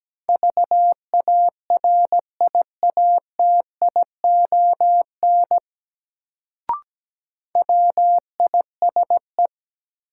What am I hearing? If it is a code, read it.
Morse code, heard as a keyed tone, about 17 wpm, 700 Hz. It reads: VARIATION WISE
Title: Sentences from Top 1600 Words - New Word Drill Alt - 17wpm